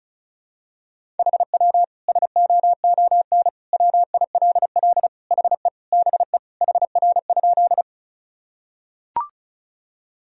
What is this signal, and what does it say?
Morse code, heard as a keyed tone, about 34 wpm, 700 Hz. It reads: HW SOON WILL HE BE HR?
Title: Sentences from Top 400 Words and Abbreviations 35wpm